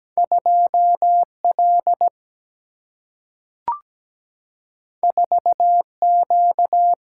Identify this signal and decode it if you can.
Morse code, heard as a keyed tone, about 17 wpm, 700 Hz. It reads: 2L 4Q
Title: Single-Letter Single-Number 17wpm